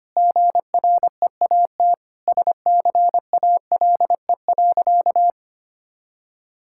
Morse code, heard as a keyed tone, about 25 wpm, 700 Hz. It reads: GREAT SCALE.
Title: Sentences from Top 1000 Words - Review Part 5 - 25wpm